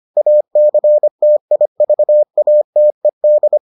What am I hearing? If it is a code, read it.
Morse code, heard as a keyed tone, about 25 wpm, 600 Hz. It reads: ACTIVATED